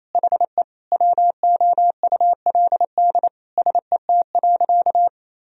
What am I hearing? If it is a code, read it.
Morse code, heard as a keyed tone, about 28 wpm, 700 Hz. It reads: HE WOULD SET.
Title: Sentences from Top 1000 Words - Review Part 1 - 28wpm